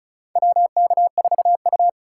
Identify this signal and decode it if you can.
Morse code, heard as a keyed tone, about 34 wpm, 700 Hz. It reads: WK4U
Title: USA Call Signs - 2x Character Spacing - 35wpm